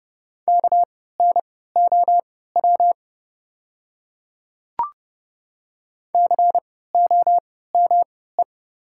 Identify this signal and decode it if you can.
Morse code, heard as a keyed tone, about 30 wpm, spaced as for 17 wpm, 700 Hz. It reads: KNOW COME
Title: 3x Character Spacing - Top 100 Words - 30wpm